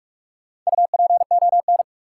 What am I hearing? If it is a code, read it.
Morse code, heard as a keyed tone, about 46 wpm, 700 Hz. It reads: UPON
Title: Sentences from Top 1100 Words - New Word Drill Alt - 45wpm